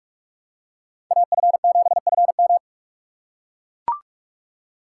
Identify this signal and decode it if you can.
Morse code, heard as a keyed tone, about 43 wpm, 700 Hz. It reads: AF7FM
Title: USA Call Signs - 4x Character Spacing - 45wpm